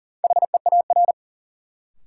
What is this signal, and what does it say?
Morse code, heard as a keyed tone, about 40 wpm, 700 Hz. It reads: HEAR
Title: Sentences from Top 300 Words - New Word Drill - 40wpm